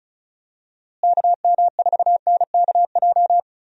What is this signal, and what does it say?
Morse code, heard as a keyed tone, about 36 wpm, 700 Hz. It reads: KM4NKJ
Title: USA Call Signs - 6 Characters - Repeated 2x - 35wpm